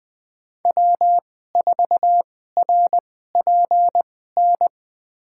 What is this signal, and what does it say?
Morse code, heard as a keyed tone, about 20 wpm, spaced as for 15 wpm, 700 Hz. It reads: W4RPN
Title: USA Call Signs - 2x Character Spacing - 20wpm